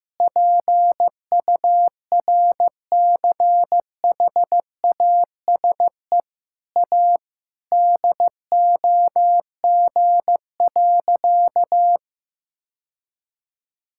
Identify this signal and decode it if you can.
Morse code, heard as a keyed tone, about 15 wpm, 700 Hz. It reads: PURCHASE A DOG.
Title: Sentences from Top 1000 Words - Review Part 5 - 15wpm